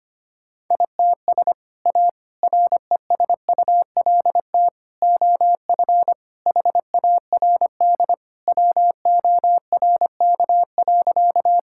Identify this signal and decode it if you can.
Morse code, heard as a keyed tone, about 25 wpm, 700 Hz. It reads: ITS A RESULT OF HARD WORK.
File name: practice / 25wpm / Sentences with 3-Word Phrases - 25wpm